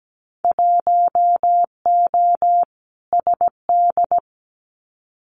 Morse code, heard as a keyed tone, about 17 wpm, 700 Hz. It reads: 1O SD
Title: Winter Field Day Exchanges Encore 17wpm